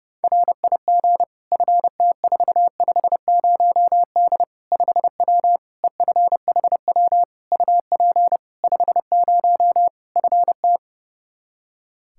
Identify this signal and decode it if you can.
Morse code, heard as a keyed tone, about 30 wpm, 700 Hz. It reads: RIG FT450D 5W EFHW UP 50 FT
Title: Practice QSO - Exchange 2 - Info - 30wpm